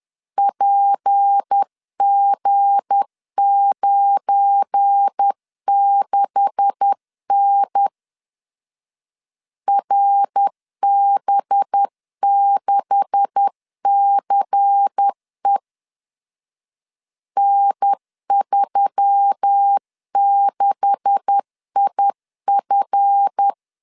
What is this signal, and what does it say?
Morse code, heard as a keyed tone, about 10 wpm, 800 Hz. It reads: PG96N RB6CE N36IF
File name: LEZIONI / 9 286 10wpm 2735